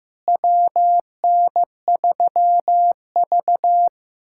Morse code, heard as a keyed tone, about 15 wpm, 700 Hz. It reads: WN3V
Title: USA Callsign DE Callsign - 4 Characters - 15wpm